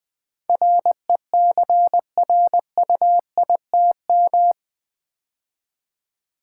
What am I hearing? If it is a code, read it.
Morse code, heard as a keyed tone, about 20 wpm, 700 Hz. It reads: RECRUITM
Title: Build A Word - 4 Letters At A Time - 11-Letter Words - 20wpm